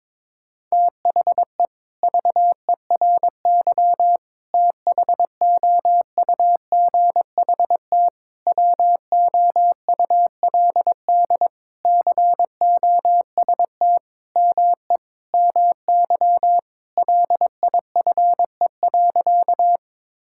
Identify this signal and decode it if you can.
Morse code, heard as a keyed tone, about 22 wpm, 700 Hz. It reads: THE VERY THOUGHT WOULD COST ME MY LIFE.
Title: Sentences from Top 1000 Words - Review Part 5 - 22wpm